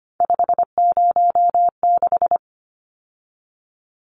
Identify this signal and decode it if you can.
Morse code, heard as a keyed tone, about 25 wpm, 700 Hz. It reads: 506
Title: Numbers - 3-Digit 25wpm